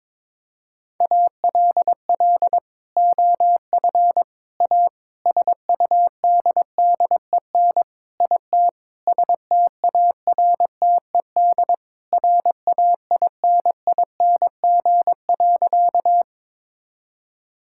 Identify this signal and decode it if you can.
Morse code, heard as a keyed tone, about 22 wpm, 700 Hz. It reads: ALL OF A SUDDEN IT STARTED RAINING.
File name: practice / 22wpm / Sentences with 4-Word Phrases - 22wpm